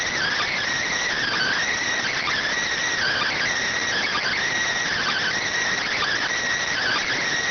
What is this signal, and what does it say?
Emergency Alert System (SAME) tones (header bursts), heard as harsh buzzing bursts. It is heard from 1.1 to 7.2 s.